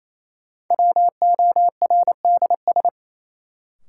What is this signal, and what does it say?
Morse code, heard as a keyed tone, about 28 wpm, 700 Hz. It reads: WORDS
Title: Sentences from Top 1000 Words - Review Part 6 - New Word Drill - 28wpm